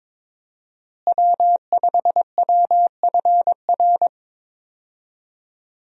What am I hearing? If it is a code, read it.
Morse code, heard as a keyed tone, about 21 wpm, 700 Hz. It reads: W5WFR
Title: USA Call Signs - 5 Characters - Repeated 3x - 22wpm